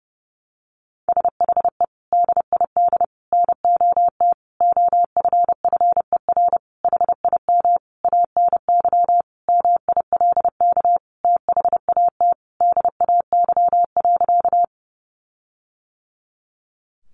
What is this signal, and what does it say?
Morse code, heard as a keyed tone, about 30 wpm, 700 Hz. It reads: SHE DID NOT OFFER HIM ANY MILK THAT DAY.
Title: Sentences from Top 1500 Words 30wpm